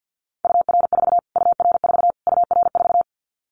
Morse code, heard as a keyed tone, about 50 wpm, 700 Hz. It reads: UR4 UR4 UR4
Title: Callsign Prefixes for Top 25 DXCC - Repeated 3x - 50wpm